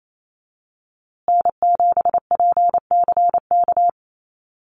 Morse code, heard as a keyed tone, about 28 wpm, 700 Hz. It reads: N7PCK
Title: USA Call Signs Repeated 2x 28wpm